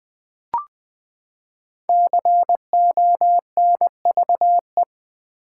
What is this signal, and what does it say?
Morse code, heard as a keyed tone, about 20 wpm, 700 Hz. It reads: CONVE
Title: Build A Word - 5 Letters At A Time - 10-Letter Words - 20wpm